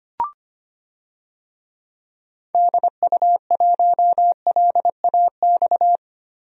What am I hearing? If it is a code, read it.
Morse code, heard as a keyed tone, about 25 wpm, 700 Hz. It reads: DU1LAX
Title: DXCC-100 Call Signs - 6 Characters - 25wpm